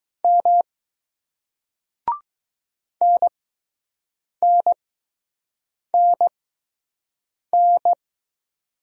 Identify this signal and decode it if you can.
Morse code, heard as a keyed tone, about 20 wpm, spaced as for 6 wpm, 700 Hz. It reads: M NNNN